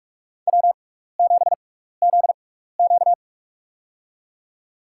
Morse code, heard as a keyed tone, about 43 wpm, spaced as for 15 wpm, 700 Hz. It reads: W7ZQ
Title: USA Call Signs - 6x Character Spacing - 45wpm